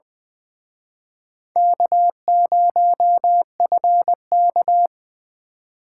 Morse code, heard as a keyed tone, about 23 wpm, 700 Hz. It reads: K0FK
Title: Speed Racing - USA 1x2 Call Signs 20wpm